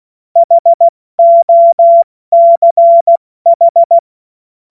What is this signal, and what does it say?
Morse code, heard as a keyed tone, about 16 wpm, spaced as for 14 wpm, 680 Hz. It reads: HOCH
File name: Morsekurs / Witze / Witz 13. September 2022 70 F80